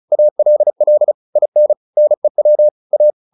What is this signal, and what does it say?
Morse code, heard as a keyed tone, about 34 wpm, 600 Hz. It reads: ALL IN NEW A